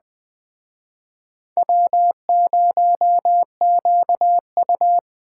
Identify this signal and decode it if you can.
Morse code, heard as a keyed tone, about 20 wpm, 700 Hz. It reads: W0QU